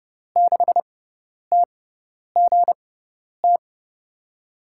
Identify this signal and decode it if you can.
Morse code, heard as a keyed tone, about 32 wpm, spaced as for 10 wpm, 700 Hz. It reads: 6TGT
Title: USA Call Signs - 6x Character Spacing - 30wpm